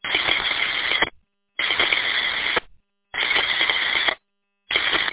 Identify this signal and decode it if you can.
Emergency Alert System (SAME) tones, heard as harsh buzzing bursts. It is heard from 0.5 to 4.0 s.